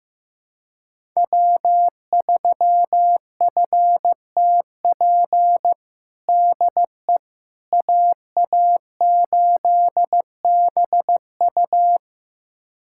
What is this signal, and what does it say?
Morse code, heard as a keyed tone, about 15 wpm, 700 Hz. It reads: W3FTP DE AA8BU
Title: USA Callsign DE Callsign - 5 Characters - 15wpm